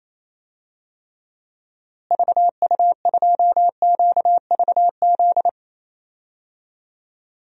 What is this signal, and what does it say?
Morse code, heard as a keyed tone, about 28 wpm, 700 Hz. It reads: VU2QVZ
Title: DXCC-100 Call Signs - 6 Characters - Repeated 3x - 28wpm